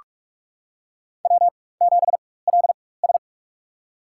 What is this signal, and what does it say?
Morse code, heard as a keyed tone, about 41 wpm, spaced as for 20 wpm, 700 Hz. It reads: W7LS
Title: USA Call Signs - 4 Characters - 4x Character Spacing - 45wpm